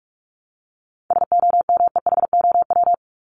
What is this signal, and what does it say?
Morse code, heard as a keyed tone, about 44 wpm, 700 Hz. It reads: SOMEHOW
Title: Top 1501-1750 Words 45wpm